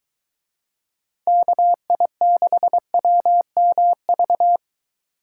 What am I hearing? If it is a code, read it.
Morse code, heard as a keyed tone, about 23 wpm, 700 Hz. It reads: KI6WMV